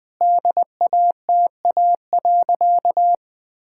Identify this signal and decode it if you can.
Morse code, heard as a keyed tone, about 21 wpm, 700 Hz. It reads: DATA.
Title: Sentences with 6-Word Phrases - Spoken Beginning - 20wpm